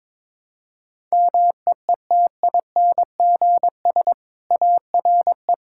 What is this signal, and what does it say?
Morse code, heard as a keyed tone, about 22 wpm, 700 Hz. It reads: MEETINGS ARE
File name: practice / 22wpm / Easy Sentences with Hints - 22wpm